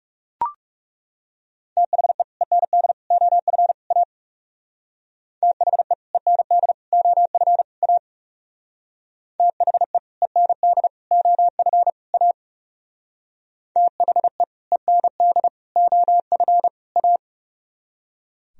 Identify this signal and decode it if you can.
Morse code, heard as a keyed tone, about 36 wpm, 700 Hz. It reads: THE END OF A THE END OF A THE END OF A THE END OF A
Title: Speed-Racing - Sets of 4 Words 30wpm